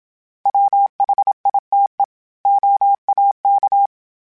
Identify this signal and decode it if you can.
Morse code, heard as a keyed tone, about 27 wpm, 800 Hz. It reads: WHITE OAK
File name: fromSMs / FromSMs mp3s / Cities@25